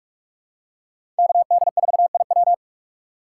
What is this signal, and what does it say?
Morse code, heard as a keyed tone, about 45 wpm, 700 Hz. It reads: KD4IW
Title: USA Call Signs - 5 Characters - 3x Character Spacing - 45wpm